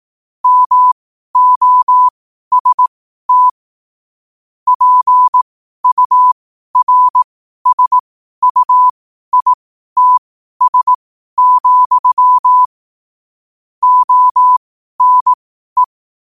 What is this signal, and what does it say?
Morse code, heard as a keyed tone, about 18 wpm, spaced as for 12 wpm, 1000 Hz. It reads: MOST PURSUITS, ONE
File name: qotdfiles / 12wpm